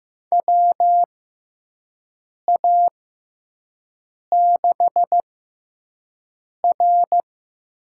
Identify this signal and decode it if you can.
Morse code, heard as a keyed tone, about 15 wpm, spaced as for 5 wpm, 700 Hz. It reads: WA6R
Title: USA Call Signs - 5 Characters - 6x Character Spacing - 15wpm